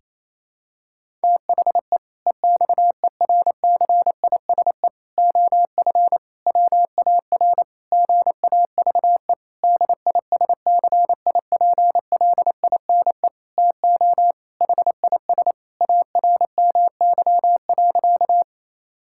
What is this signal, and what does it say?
Morse code, heard as a keyed tone, about 28 wpm, 700 Hz. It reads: THE EXERCISE OF WAR GAVE DISCIPLINE TO HIS ARMY.
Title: Sentences from Top 1000 Words - Review Part 6 - 28wpm